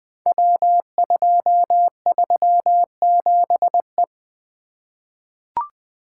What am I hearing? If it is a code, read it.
Morse code, heard as a keyed tone, about 20 wpm, 700 Hz. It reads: W237E
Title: Code Groups - 5 Letter:Numbers - 20wpm